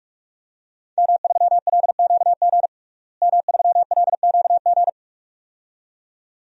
Morse code, heard as a keyed tone, about 48 wpm, 700 Hz. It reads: M3LQG M3LQG
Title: DXCC-100 Call Signs - 5 Characters - Repeated 2x - 45wpm